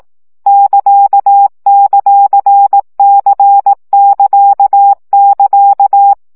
Morse code, heard as a keyed tone, about 18 wpm, 800 Hz. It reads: <KA>;C<KA><KA>
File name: CW gyakorlas c ct pontosvesszo 15 baud